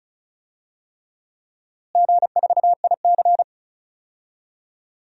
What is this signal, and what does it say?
Morse code, heard as a keyed tone, about 36 wpm, 700 Hz. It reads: G4IC
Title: DXCC-100 Call Signs - 4 Characters - Repeated 3x - 35wpm